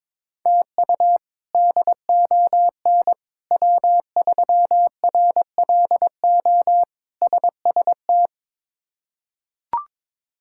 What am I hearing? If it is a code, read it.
Morse code, heard as a keyed tone, about 22 wpm, 700 Hz. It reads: TU DON W3RLO SST
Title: K1USN SST QSO Practice - 22wpm